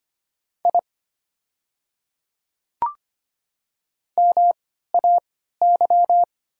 Morse code, heard as a keyed tone, about 25 wpm, spaced as for 14 wpm, 700 Hz. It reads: I MAY
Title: 3x Character Spacing - Top 200 Words - 25wpm